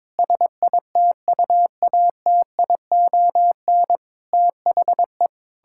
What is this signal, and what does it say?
Morse code, heard as a keyed tone, about 21 wpm, 700 Hz. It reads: SITUATION THE
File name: Sentences with 9-Word Phrases - 22wpm